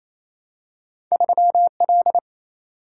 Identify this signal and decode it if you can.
Morse code, heard as a keyed tone, about 27 wpm, 700 Hz. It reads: 3L
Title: Single-Letter Single-Number 28wpm